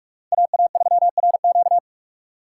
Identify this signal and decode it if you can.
Morse code, heard as a keyed tone, about 42 wpm, 700 Hz. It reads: AA3RQ
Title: USA Call Signs - 3x Character Spacing - 45wpm